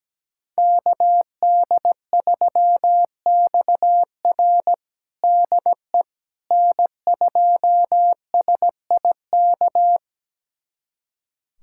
Morse code, heard as a keyed tone, about 17 wpm, 700 Hz. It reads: KD3XR DE N2SIK
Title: USA Callsign DE Callsign - 5 Characters - 17wpm